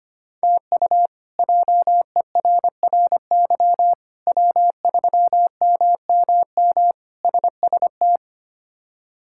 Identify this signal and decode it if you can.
Morse code, heard as a keyed tone, about 25 wpm, 700 Hz. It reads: TU JERRY W3MMM SST
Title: K1USN SST QSO Practice - 25wpm